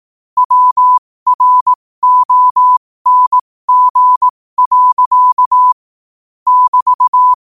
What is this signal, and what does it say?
Morse code, heard as a keyed tone, about 18 wpm, spaced as for 15 wpm, 1000 Hz. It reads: WRONG. <BT>
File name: qotdfiles / 15wpm